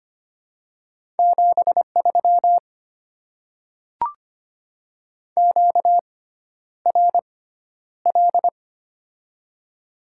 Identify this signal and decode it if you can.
Morse code, heard as a keyed tone, about 25 wpm, spaced as for 11 wpm, 700 Hz. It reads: 73 QRL
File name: Top 20 QSO Elements - 6x Character Spacing - 25wpm